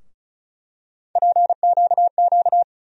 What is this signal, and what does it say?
Morse code, heard as a keyed tone, about 36 wpm, 700 Hz. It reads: PQQ
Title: USA Call Sign Postfixes - 3 Letters 35wpm